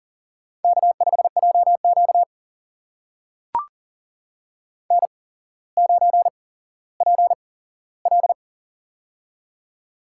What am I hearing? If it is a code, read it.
Morse code, heard as a keyed tone, about 38 wpm, spaced as for 17 wpm, 700 Hz. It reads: K5JQ N9PL
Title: USA Call Signs - 4 Characters - 8x Character Spacing - 40wpm